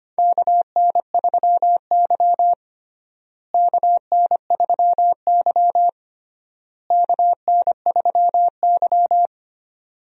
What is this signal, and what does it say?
Morse code, heard as a keyed tone, about 25 wpm, 700 Hz. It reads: KN3Y KN3Y KN3Y
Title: USA Call Signs - 4 Characters - Repeated 3x - 25wpm